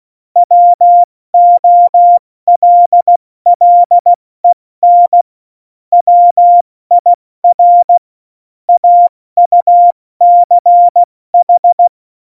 Morse code, heard as a keyed tone, about 16 wpm, 700 Hz. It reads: WOLLEN WIR AUCH
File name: Morsekurs / Witze / Witz 07. November 2017, 70 F80